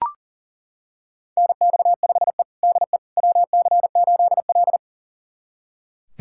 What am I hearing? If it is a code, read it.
Morse code, heard as a keyed tone, about 37 wpm, 700 Hz. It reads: NX5E DE WC8L
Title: USA Callsign DE Callsign - 4 Characters - 40wpm